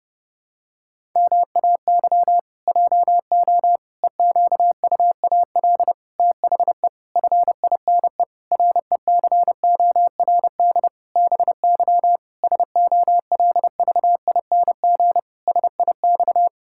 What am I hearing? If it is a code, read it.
Morse code, heard as a keyed tone, about 28 wpm, 700 Hz. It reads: MAY JO EQUAL THE FINE RECORD BY SOLVING SIX
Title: Pangram Challenge v2 - 30wpm